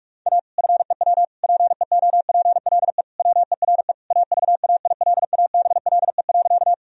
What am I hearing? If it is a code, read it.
Morse code, heard as a keyed tone, about 46 wpm, 700 Hz. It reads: A FEW PEOPLE WERE AVAILABLE.
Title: Sentences from Top 1000 Words - Review Part 3 - 45wpm